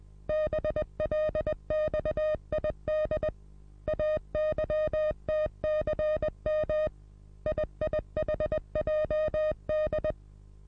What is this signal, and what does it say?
Morse code, heard as a keyed tone, about 20 wpm, 620 Hz. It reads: BLXID AYTCM IIHJD
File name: repos / Morsekurs / DJ4UF Morsen27